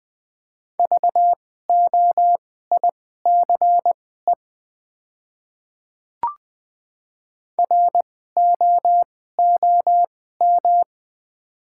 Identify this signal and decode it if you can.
Morse code, heard as a keyed tone, about 20 wpm, spaced as for 15 wpm, 690 Hz. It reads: VOICE ROOM
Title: 2x Character Spacing - Top 500 Words - 20wpm